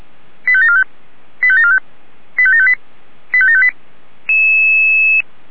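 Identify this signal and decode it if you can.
Five-tone selective call (ZVEI), heard as quick runs of five tones, tones of about 70 ms. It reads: 85745 85745 85758 85758
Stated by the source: Five-tone ZVEI selective call received by SDR.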